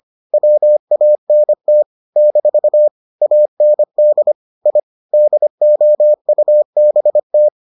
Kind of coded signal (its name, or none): Morse code